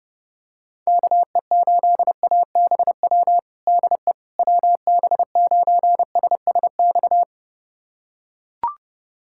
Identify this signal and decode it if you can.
Morse code, heard as a keyed tone, about 30 wpm, 700 Hz. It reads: KE8ABW DE WB9SSX
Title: USA Callsign DE Callsign - 6 Characters - 30wpm